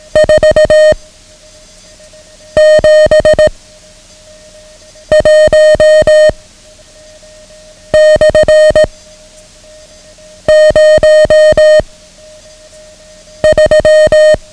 Morse code, heard as a keyed tone, about 18 wpm, 620 Hz. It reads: 4 7 1 / 0 3
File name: ham radio / CodeQuick / Code Quick 09